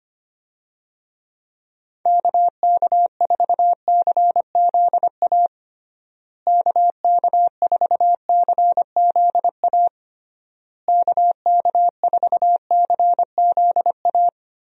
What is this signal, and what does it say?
Morse code, heard as a keyed tone, about 25 wpm, 700 Hz. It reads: KK4CZA KK4CZA KK4CZA
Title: USA Call Signs - 6 Characters - Repeated 3x - 25wpm